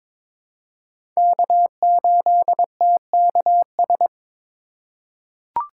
Morse code, heard as a keyed tone, about 21 wpm, 700 Hz. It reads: K8TKS
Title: USA Call Signs - 5 Characters - 6x Character Spacing - 22wpm